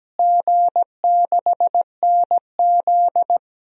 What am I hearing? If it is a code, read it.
Morse code, heard as a keyed tone, about 17 wpm, 700 Hz. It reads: G6NZ
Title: DXCC-100 Call Signs - 4 Characters - Repeated 2x - 17wpm